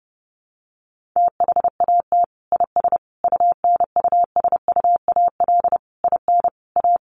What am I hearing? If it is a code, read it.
Morse code, heard as a keyed tone, about 30 wpm, 700 Hz. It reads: THAT IS UNUSUAL IN A
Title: Sentences from Top 1400 Words 30wpm